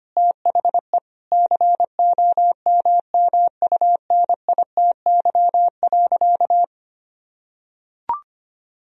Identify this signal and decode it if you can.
Morse code, heard as a keyed tone, about 25 wpm, 700 Hz. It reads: THE COMMUNITY.
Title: Sentences from Top 1000 Words - Review Part 4 - 25wpm